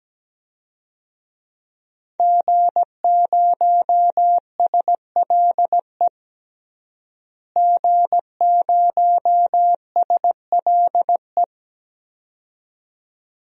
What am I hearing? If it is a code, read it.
Morse code, heard as a keyed tone, about 17 wpm, 700 Hz. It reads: G0SLE G0SLE
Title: DXCC-100 Call Signs - 5 Characters - Repeated 2x - 17wpm